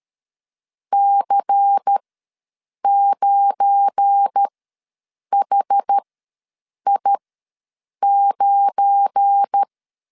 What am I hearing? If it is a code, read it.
Morse code, heard as a keyed tone, about 13 wpm, spaced as for 7 wpm, 800 Hz. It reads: C9HI9
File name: LEZIONI / 7 GF9Z / 7 GF9 ws 12wpm 3344